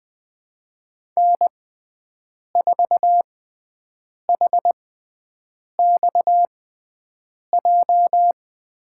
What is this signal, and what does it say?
Morse code, heard as a keyed tone, about 20 wpm, spaced as for 7 wpm, 700 Hz. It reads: N4HXJ
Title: USA Call Signs - 5 Characters - 6x Character Spacing - 20wpm